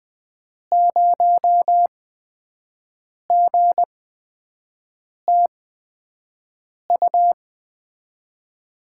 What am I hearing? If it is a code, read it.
Morse code, heard as a keyed tone, about 20 wpm, spaced as for 5 wpm, 700 Hz. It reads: 0GTU
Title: USA Call Signs - 6 Characters - 8x Character Spacing - 20wpm